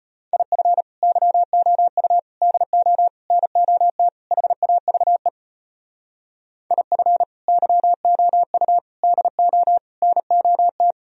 Morse code, heard as a keyed tone, about 33 wpm, 700 Hz. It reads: IF YOU DO NOT HAVE IF YOU DO NOT